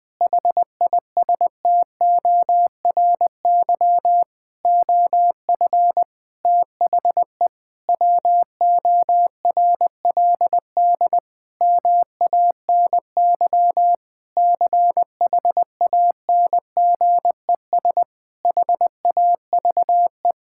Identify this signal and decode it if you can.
Morse code, heard as a keyed tone, about 20 wpm, 700 Hz. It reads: HISTORY OF THE WORLD MANY CHANGES HAVE
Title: Sentences with 6-Word Phrases - 20wpm